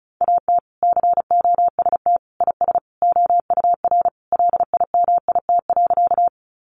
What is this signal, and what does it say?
Morse code, heard as a keyed tone, about 34 wpm, 700 Hz. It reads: AT COST IS OUR LIMIT.
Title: Sentences from Top 1500 Words 35wpm